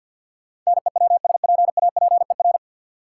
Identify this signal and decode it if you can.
Morse code, heard as a keyed tone, about 52 wpm, 700 Hz. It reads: NEWSPAPER
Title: Sentences from Top 1000 Words - Review Part 4 - New Word Drill - 50wpm